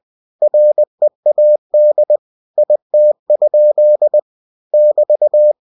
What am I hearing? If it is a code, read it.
Morse code, heard as a keyed tone, about 20 wpm, 600 Hz. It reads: READ IT? <BT>
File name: cw-books / winnie-the-pooh / winnie-20wpm-page-0052